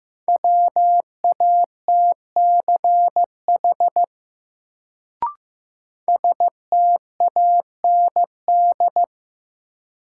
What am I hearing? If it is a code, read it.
Morse code, heard as a keyed tone, about 15 wpm, 700 Hz. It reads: WATCH STAND
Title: Sentences from Top 300 Words - New Word Drill - 15wpm